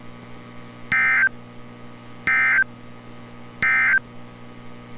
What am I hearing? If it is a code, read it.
Emergency Alert System (SAME) tones, heard as harsh buzzing bursts. It is heard from 0.8 to 4.1 s.
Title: NOAA Weather Radio - Altoona, PA